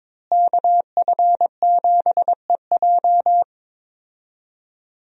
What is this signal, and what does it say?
Morse code, heard as a keyed tone, about 22 wpm, 700 Hz. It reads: KF7EJ
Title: USA Call Signs - 5 Characters - Repeated 3x - 22wpm